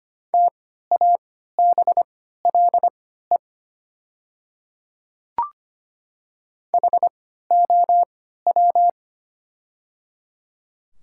Morse code, heard as a keyed tone, about 25 wpm, spaced as for 14 wpm, 700 Hz. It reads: TABLE HOW
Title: 3x Character Spacing - Top 500 Words - 25wpm